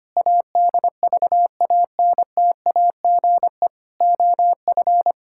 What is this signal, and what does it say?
Morse code, heard as a keyed tone, about 25 wpm, 700 Hz. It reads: ADVANTAGE OF
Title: Sentences from Top 1000 Words - Review Part 6 - 25wpm